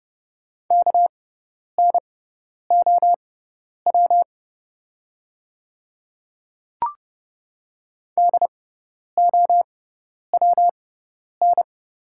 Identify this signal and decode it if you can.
Morse code, heard as a keyed tone, about 31 wpm, spaced as for 10 wpm, 690 Hz. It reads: KNOW DOWN
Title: 6x Character Spacing - Top 100 Words - 30wpm